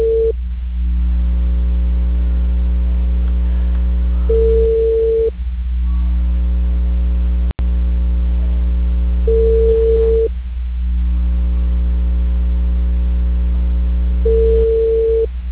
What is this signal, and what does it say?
Ringback tone (China, 450 Hz).